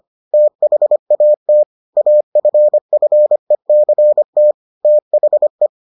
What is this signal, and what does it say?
Morse code, heard as a keyed tone, about 25 wpm, 600 Hz. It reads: THAT AFFECT THE